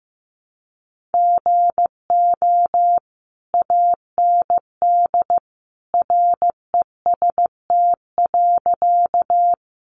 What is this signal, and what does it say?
Morse code, heard as a keyed tone, about 15 wpm, 700 Hz. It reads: GO AND REST.
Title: Sentences from Top 500 Words 15wpm